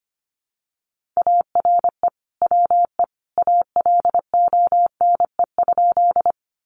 Morse code, heard as a keyed tone, about 25 wpm, 700 Hz. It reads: ARE WE ALONE?
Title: Sentences from Top 1000 Words 25wpm